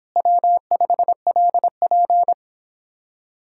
Morse code, heard as a keyed tone, about 31 wpm, 700 Hz. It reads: W5LP